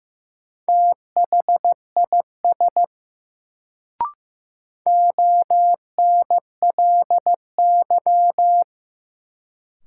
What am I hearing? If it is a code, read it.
Morse code, heard as a keyed tone, about 15 wpm, 700 Hz. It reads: THIS ONLY
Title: Sets of 2 Words - New Word Drill - 15wpm